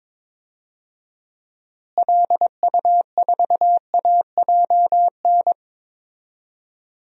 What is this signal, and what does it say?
Morse code, heard as a keyed tone, about 22 wpm, 700 Hz. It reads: LU4AJN
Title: DXCC-100 Call Signs - 6 Characters - Repeated 2x - 22wpm